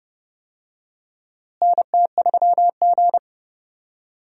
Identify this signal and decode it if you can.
Morse code, heard as a keyed tone, about 31 wpm, 700 Hz. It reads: NT3G
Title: USA Call Signs - 4 Characters - Repeated 2x - 30wpm